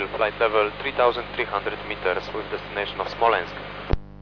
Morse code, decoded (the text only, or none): none